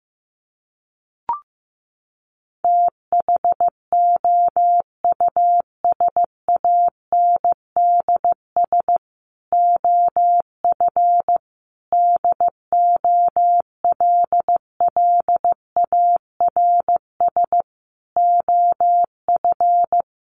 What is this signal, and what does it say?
Morse code, heard as a keyed tone, about 15 wpm, 700 Hz. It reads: THOUSANDS OF DOLLARS OF
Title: A Sound Of Thunder 15wpm